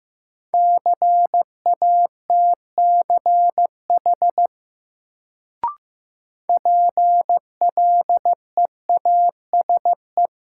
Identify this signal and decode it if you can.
Morse code, heard as a keyed tone, about 16 wpm, 700 Hz. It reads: CATCH PLEASE
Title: Sentences from Top 1500 Words - New Word Drill Alt - 15wpm